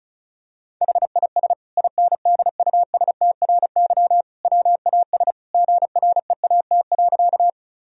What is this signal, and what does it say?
Morse code, heard as a keyed tone, about 36 wpm, 700 Hz. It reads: HIS INDUSTRY WAS GREAT.
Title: Sentences from Top 1000 Words - Review Part 4 - 35wpm